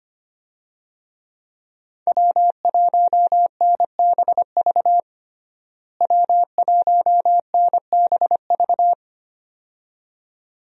Morse code, heard as a keyed tone, about 25 wpm, 700 Hz. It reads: W1NBV W1NBV
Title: USA Call Signs - 5 Characters - Repeated 2x - 25wpm